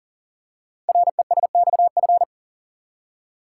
Morse code, heard as a keyed tone, about 38 wpm, 690 Hz. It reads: RESXF
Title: Code Groups - 5 Letters - 40wpm